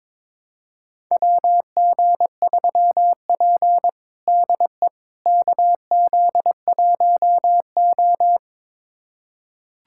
Morse code, heard as a keyed tone, about 21 wpm, 700 Hz. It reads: WG3P DE KZ1O
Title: USA Callsign DE Callsign - 4 Characters - 22wpm